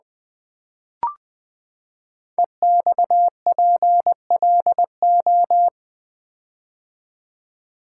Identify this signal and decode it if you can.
Morse code, heard as a keyed tone, about 20 wpm, 700 Hz. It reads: EXPLO
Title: Build A Word - 5 Letters At A Time - 12-Letter Words - 20wpm